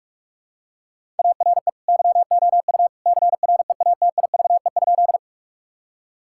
Morse code, heard as a keyed tone, about 46 wpm, 700 Hz. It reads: ARE YOU CREATIVE?
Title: Sentences from Top 1000 Words - Review Part 5 - 45wpm